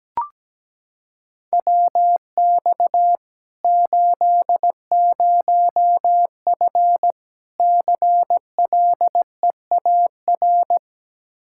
Morse code, heard as a keyed tone, about 17 wpm, 700 Hz. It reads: WX 80F CLEAR
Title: Practice QSO - Exchange 2 - Info - 17wpm